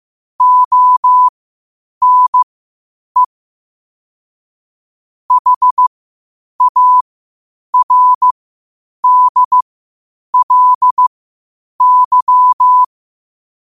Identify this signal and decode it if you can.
Morse code, heard as a keyed tone, about 15 wpm, spaced as for 8 wpm, 1000 Hz. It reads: ONE HARDLY
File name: qotdfiles / 8wpm